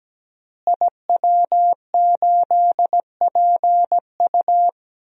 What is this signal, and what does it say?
Morse code, heard as a keyed tone, about 17 wpm, 700 Hz. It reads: IW8PU
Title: DXCC-100 Call Signs - 5 Characters - Repeated 2x - 17wpm